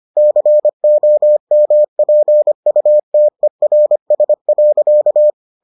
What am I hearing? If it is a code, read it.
Morse code, heard as a keyed tone, about 25 wpm, 600 Hz. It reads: COMPUTERS.